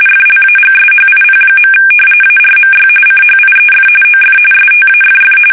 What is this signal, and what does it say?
RTTY, about 45.45 baud. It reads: FLAGSHIP. ARMY CHIEF OF STAFF
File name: fsk850 / battleshipnj